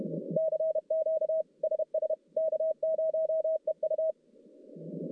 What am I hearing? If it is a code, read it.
Morse code, heard as a keyed tone, about 31 wpm, 630 Hz. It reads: CQ SS K0EU
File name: contests / CW-SS-k5tr- -1200